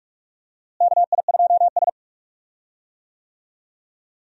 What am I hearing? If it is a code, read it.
Morse code, heard as a keyed tone, about 46 wpm, 700 Hz. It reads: KI2S
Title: USA Call Signs - 4 Characters - Repeated 2x - 45wpm